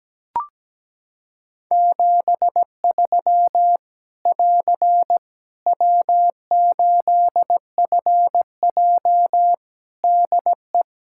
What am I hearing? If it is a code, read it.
Morse code, heard as a keyed tone, about 17 wpm, 700 Hz. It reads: 73 <AR> W8FJ DE
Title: Practice QSO - Exchange 4 - Ending - 17wpm